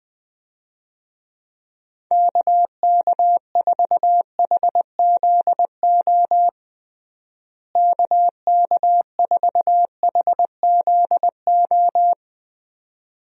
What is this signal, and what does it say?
Morse code, heard as a keyed tone, about 21 wpm, 700 Hz. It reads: KK4HZO KK4HZO
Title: USA Call Signs - 6 Characters - Repeated 2x - 20wpm